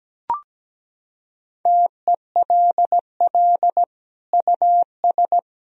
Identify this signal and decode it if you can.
Morse code, heard as a keyed tone, about 17 wpm, 700 Hz. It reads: TELL US
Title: Sentences from Top 1000 Words - Review Part 4 - 17wpm